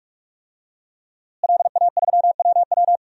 Morse code, heard as a keyed tone, about 46 wpm, 700 Hz. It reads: LA3WW
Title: DXCC-100 Call Signs - 5 Characters - Repeated 2x - 45wpm